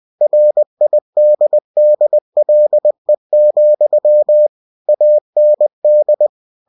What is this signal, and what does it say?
Morse code, heard as a keyed tone, about 20 wpm, 600 Hz. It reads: RIDDLE, AND
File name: cw-books / winnie-the-pooh / winnie-20wpm-page-0045